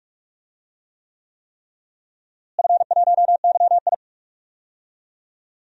Morse code, heard as a keyed tone, about 46 wpm, 700 Hz. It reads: F1YI
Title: DXCC-100 Call Signs - 4 Characters - Repeated 3x - 45wpm